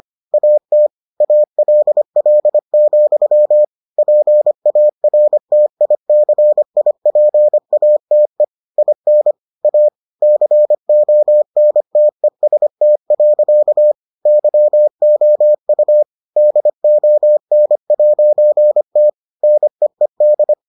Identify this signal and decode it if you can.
Morse code, heard as a keyed tone, about 25 wpm, 600 Hz. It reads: AT ALL, PARTICIPATE IN A CONTEST. YOU DON'T NEED